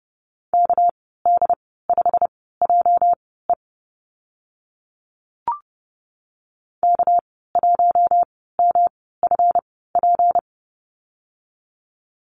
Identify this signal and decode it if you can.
Morse code, heard as a keyed tone, about 30 wpm, spaced as for 17 wpm, 700 Hz. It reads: KD5JE K1MFP
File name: Lesson 37g - Letter x - Call Signs 30wpm